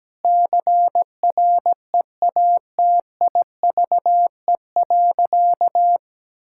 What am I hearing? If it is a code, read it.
Morse code, heard as a keyed tone, about 17 wpm, 700 Hz. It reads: CREATIVE.
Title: Sentences from Top 1000 Words - Review Part 5 - 17wpm